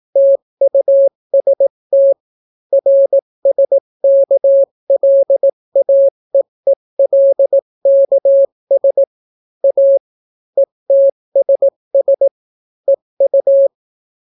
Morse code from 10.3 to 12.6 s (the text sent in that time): ETSS